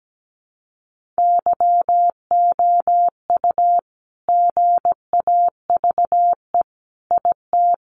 Morse code, heard as a keyed tone, about 17 wpm, 700 Hz. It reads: YOU GAVE IT
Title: Sentences from Top 1500 Words 17wpm